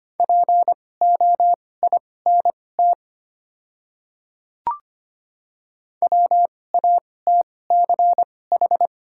Morse code, heard as a keyed tone, about 25 wpm, spaced as for 18 wpm, 700 Hz. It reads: POINT WATCH
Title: 2x Character Spacing - Top 500 Words - 25wpm